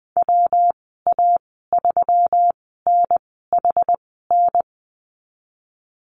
Morse code, heard as a keyed tone, about 20 wpm, spaced as for 15 wpm, 700 Hz. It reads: WA3NHN
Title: 122 Lesson 20g - Number 3 - Call Signs 20wpm